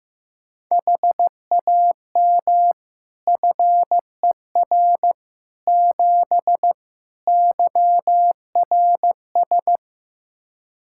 Morse code, heard as a keyed tone, about 15 wpm, 700 Hz. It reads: HAM FER 7 YRS
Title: Practice QSO - Exchange 3 - More Info - 15wpm